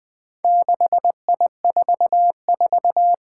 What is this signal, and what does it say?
Morse code, heard as a keyed tone, about 20 wpm, 700 Hz. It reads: 6I44
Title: Four Challenging Letter-Number Combinations 20wpm